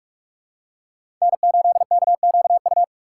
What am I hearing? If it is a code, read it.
Morse code, heard as a keyed tone, about 46 wpm, 700 Hz. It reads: N8KQU
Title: USA Call Signs - 5 Characters - 3x Character Spacing - 45wpm